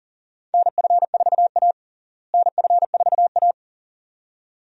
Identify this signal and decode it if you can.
Morse code, heard as a keyed tone, about 40 wpm, 700 Hz. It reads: NF4A NF4A
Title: USA Call Signs - 4 Characters - Repeated 2x - 40wpm